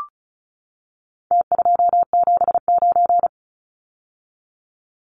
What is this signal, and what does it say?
Morse code, heard as a keyed tone, about 34 wpm, 700 Hz. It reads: T279
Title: Numbers - Contest Style - 4-Digit 35wpm